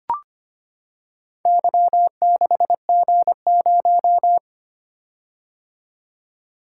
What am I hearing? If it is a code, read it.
Morse code, heard as a keyed tone, about 25 wpm, 700 Hz. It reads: Y6G0
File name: Four Letter-Number Combinations - 25wpm